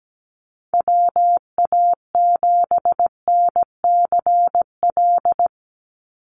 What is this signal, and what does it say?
Morse code, heard as a keyed tone, about 17 wpm, 700 Hz. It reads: WA7NCL
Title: NA SOTA Activator Callsigns 17wpm